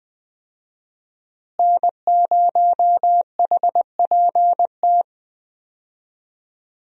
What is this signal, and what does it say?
Morse code, heard as a keyed tone, about 21 wpm, 700 Hz. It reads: N0HPT
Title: USA Call Signs - 5 Characters - Repeated 3x - 20wpm